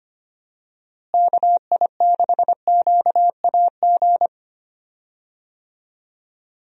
Morse code, heard as a keyed tone, about 25 wpm, 700 Hz. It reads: KI6QAG
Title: USA Call Signs - 6 Characters - Repeated 3x - 25wpm